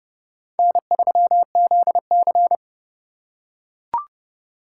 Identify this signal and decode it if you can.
Morse code, heard as a keyed tone, about 30 wpm, 700 Hz. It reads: N3ZC
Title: USA Call Signs - 4 Characters - 4x Character Spacing - 30wpm